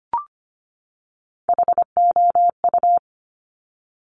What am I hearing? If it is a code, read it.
Morse code, heard as a keyed tone, about 25 wpm, 700 Hz. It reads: HOU
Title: Top Three Letter Combinations in Words 25wpm